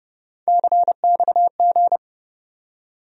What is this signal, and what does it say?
Morse code, heard as a keyed tone, about 30 wpm, 700 Hz. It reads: CXG
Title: USA Call Sign Postfixes - 3 Letters 30wpm